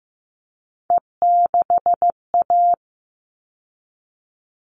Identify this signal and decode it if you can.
Morse code, heard as a keyed tone, about 15 wpm, 700 Hz. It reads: E6A
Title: All Cut Numbers - 3-Digit 15wpm